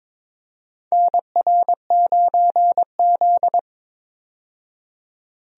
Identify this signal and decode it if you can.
Morse code, heard as a keyed tone, about 22 wpm, 700 Hz. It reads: NR9Z
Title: USA Call Signs - 4 Characters - 22wpm